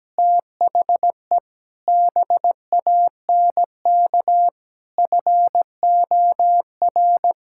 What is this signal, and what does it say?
Morse code, heard as a keyed tone, about 17 wpm, 700 Hz. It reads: THE BANK FOR
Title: Sentences from Top 1000 Words - Review Part 6 - 17wpm